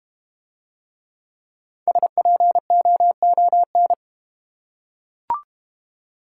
Morse code, heard as a keyed tone, about 31 wpm, 700 Hz. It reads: SPOON
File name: Easy Words - Speed-Racing with Hints - 25wpm